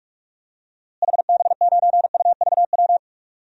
Morse code, heard as a keyed tone, about 46 wpm, 700 Hz. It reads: HB9UVW
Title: DXCC-100 Call Signs - 6 Characters - Repeated 2x - 45wpm